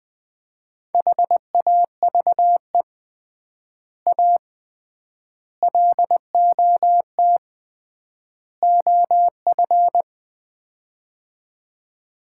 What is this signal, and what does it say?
Morse code, heard as a keyed tone, about 21 wpm, 700 Hz. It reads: HAVE A LOT OF
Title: 3x Word-Spacing - Sets of 4 Words - 20wpm